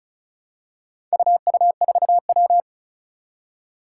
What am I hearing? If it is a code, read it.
Morse code, heard as a keyed tone, about 36 wpm, 690 Hz. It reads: UU4W
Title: Four Challenging Letter-Number Combinations 35wpm